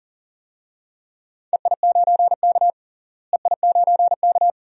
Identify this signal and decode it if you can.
Morse code, heard as a keyed tone, about 40 wpm, 700 Hz. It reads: EI9K EI9K
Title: DXCC-100 Call Signs - 4 Characters - Repeated 2x - 40wpm